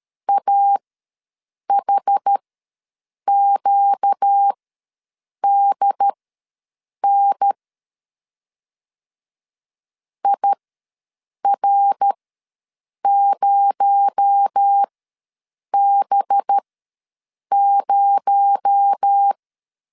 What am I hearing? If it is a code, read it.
Morse code, heard as a keyed tone, about 13 wpm, spaced as for 7 wpm, 800 Hz. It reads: AHQDN IR0B0
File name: LEZIONI / 6 PKX0 / 6 PKX0 ws 12wpm 3448